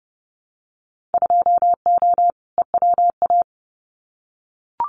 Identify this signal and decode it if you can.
Morse code, heard as a keyed tone, about 30 wpm, 700 Hz. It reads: 2O EWA
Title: Winter Field Day Exchanges Encore 30wpm